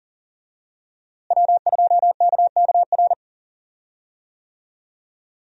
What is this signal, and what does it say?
Morse code, heard as a keyed tone, about 37 wpm, 700 Hz. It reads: W2KKR
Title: USA Call Signs - 5 Characters - Repeated 2x - 40wpm